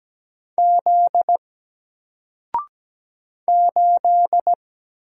Morse code, heard as a keyed tone, about 17 wpm, 700 Hz. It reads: Z 8
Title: Single Letter-Number Alt 17wpm